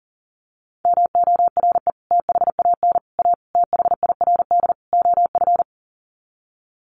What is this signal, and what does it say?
Morse code, heard as a keyed tone, about 37 wpm, 700 Hz. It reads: MORE THAN A THIRD OF
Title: Sets of 5 Words Encore2 40wpm